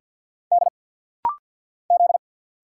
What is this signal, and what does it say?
Morse code, heard as a keyed tone, about 50 wpm, 700 Hz. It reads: D Z
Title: Single Letters - Rapid-Fire 50wpm